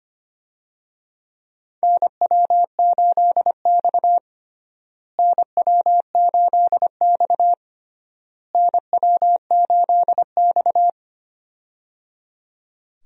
Morse code, heard as a keyed tone, about 25 wpm, 700 Hz. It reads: NW8X NW8X NW8X
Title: USA Call Signs - 4 Characters - Repeated 3x - 25wpm